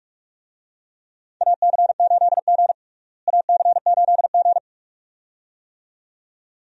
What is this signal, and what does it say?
Morse code, heard as a keyed tone, about 44 wpm, 700 Hz. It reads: AC8G AC8G
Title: USA Call Signs - 4 Characters - Repeated 2x - 45wpm